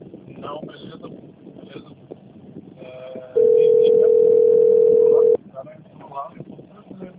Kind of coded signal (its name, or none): telephone call progress tone